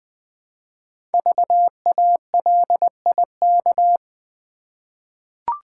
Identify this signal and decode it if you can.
Morse code, heard as a keyed tone, about 20 wpm, 700 Hz. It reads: VALIK
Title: Code Groups - 5 Letters - 20wpm